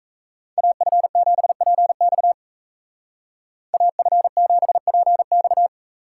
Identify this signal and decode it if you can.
Morse code, heard as a keyed tone, about 34 wpm, 700 Hz. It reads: AF7PX AF7PX
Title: Speed Racing - USA 2x2 Call Signs 28wpm